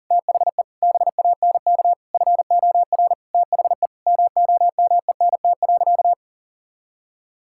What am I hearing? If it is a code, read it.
Morse code, heard as a keyed tone, about 40 wpm, 700 Hz. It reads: THE BANK FOR THE MOMENT.
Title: Sentences from Top 1000 Words - Review Part 6 - 40wpm